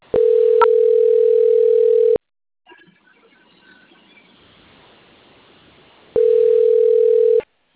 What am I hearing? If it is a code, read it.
Ringback tone (North America), 440 + 480 Hz.